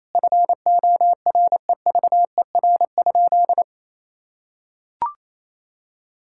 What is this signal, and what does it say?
Morse code, heard as a keyed tone, about 28 wpm, 700 Hz. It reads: FOREVER?
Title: Sentences from Top 1600 Words 28wpm